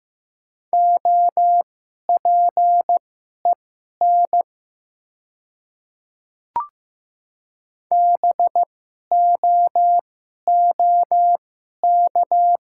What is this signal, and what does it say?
Morse code, heard as a keyed tone, about 15 wpm, spaced as for 11 wpm, 700 Hz. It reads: OPEN BOOK
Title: 2x Character Spacing - Top 300 Words - 15wpm